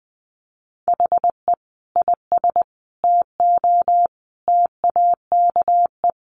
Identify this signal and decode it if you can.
Morse code, heard as a keyed tone, about 20 wpm, 700 Hz. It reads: HE IS TO TAKE
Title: Sentences from Top 1400 Words 20wpm